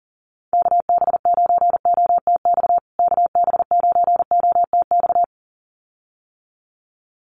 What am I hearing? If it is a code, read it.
Morse code, heard as a keyed tone, about 40 wpm, 700 Hz. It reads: KB9OTX KB9OTX
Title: USA Call Signs Repeated 2x 40wpm